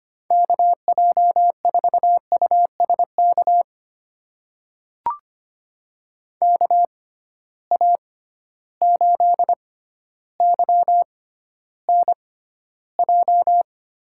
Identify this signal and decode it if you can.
Morse code, heard as a keyed tone, about 25 wpm, spaced as for 13 wpm, 700 Hz. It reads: KJ4USK KA8YNJ
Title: USA Call Signs - 6 Characters - 6x Character Spacing - 25wpm